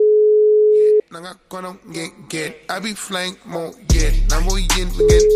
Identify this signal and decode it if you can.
Ringback tone (Europe, 425 Hz), with speech.